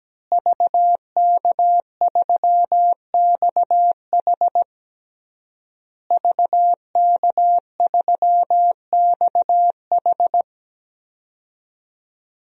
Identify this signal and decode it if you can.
Morse code, heard as a keyed tone, about 17 wpm, 700 Hz. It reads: VK3XH VK3XH
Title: DXCC-100 Call Signs - 5 Characters - Repeated 3x - 17wpm